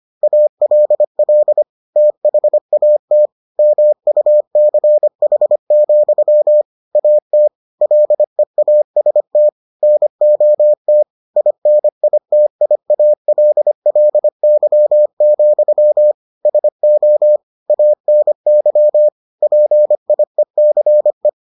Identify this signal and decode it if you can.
Morse code, heard as a keyed tone, about 25 wpm, 600 Hz. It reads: ALL THAT MUCH, AT LEAST NOT INITIALLY, SO ANY PIECE